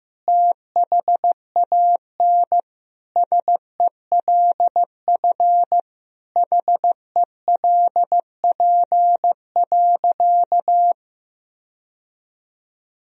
Morse code, heard as a keyed tone, about 15 wpm, 700 Hz. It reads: THAN SELF HELP.
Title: Sentences from Top 1000 Words - Review Part 1 - 15wpm